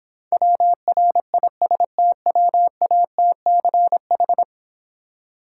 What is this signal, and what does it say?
Morse code, heard as a keyed tone, about 26 wpm, 700 Hz. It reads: WRISTWATCH